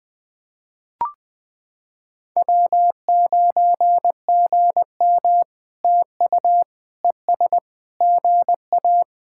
Morse code, heard as a keyed tone, about 20 wpm, 700 Hz. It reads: W9GM TU ES GA
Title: POTA QSO Practice - 20wpm